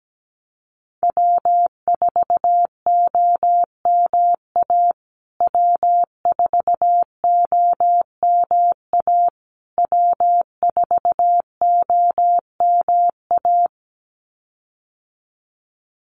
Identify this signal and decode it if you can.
Morse code, heard as a keyed tone, about 17 wpm, 700 Hz. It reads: W4OMA W4OMA W4OMA
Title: USA Call Signs Repeated 3x 17wpm